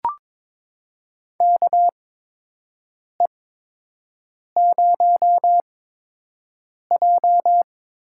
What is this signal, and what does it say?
Morse code, heard as a keyed tone, about 22 wpm, spaced as for 6 wpm, 700 Hz. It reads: KE0J
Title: USA Call Signs - 6 Characters - 8x Character Spacing - 22wpm